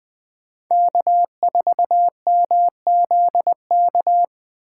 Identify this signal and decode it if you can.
Morse code, heard as a keyed tone, about 20 wpm, 700 Hz. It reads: K4MZK
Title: USA Call Signs - 5 Characters - 4x Character Spacing - 20wpm